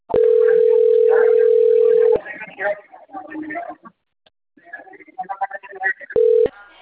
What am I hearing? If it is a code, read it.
Ringback tone (North America), 440 + 480 Hz, with speech.
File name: out- - - - - .20239